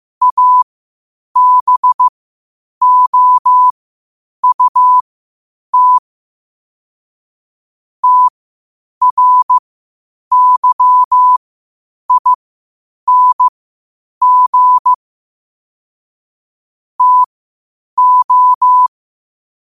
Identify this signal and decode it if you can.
Morse code, heard as a keyed tone, about 15 wpm, spaced as for 8 wpm, 1000 Hz. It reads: ABOUT TRYING TO
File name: qotdfiles / 8wpm